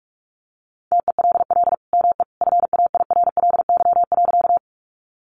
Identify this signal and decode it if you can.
Morse code, heard as a keyed tone, about 44 wpm, 700 Hz. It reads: TELL ME FAIRLY.
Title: Sentences from Top 1100 Words 45wpm